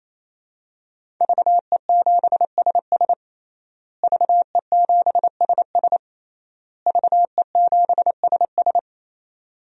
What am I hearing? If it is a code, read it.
Morse code, heard as a keyed tone, about 28 wpm, 700 Hz. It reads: VE7SS VE7SS VE7SS
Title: DXCC-100 Call Signs - 5 Characters - Repeated 3x - 28wpm